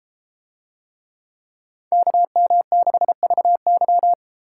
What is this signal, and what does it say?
Morse code, heard as a keyed tone, about 32 wpm, 700 Hz. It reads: KM6VY